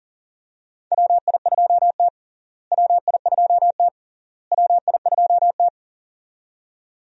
Morse code, heard as a keyed tone, about 40 wpm, 700 Hz. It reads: WI2T WI2T WI2T
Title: USA Call Signs - 4 Characters - Repeated 3x - 40wpm